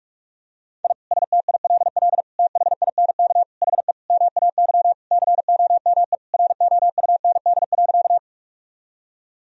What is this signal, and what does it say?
Morse code, heard as a keyed tone, about 46 wpm, 700 Hz. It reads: I STILL THINK HE MAY COME ROUND.
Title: Sentences from Top 1000 Words - Review Part 1 - 45wpm